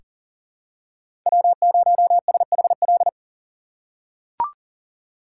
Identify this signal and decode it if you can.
Morse code, heard as a keyed tone, about 37 wpm, 700 Hz. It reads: W0SHL
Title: K1USN SST QSO Practice - 40wpm